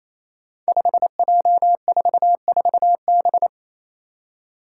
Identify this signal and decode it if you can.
Morse code, heard as a keyed tone, about 28 wpm, 700 Hz. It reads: 5J44B
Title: Five Challenging Letter-Number Combinations 28wpm